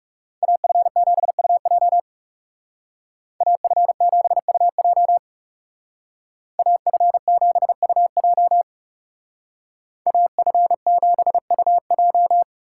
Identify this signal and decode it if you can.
Morse code, heard as a keyed tone, about 39 wpm, 700 Hz. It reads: AF7UJ AF7UJ AF7UJ AF7UJ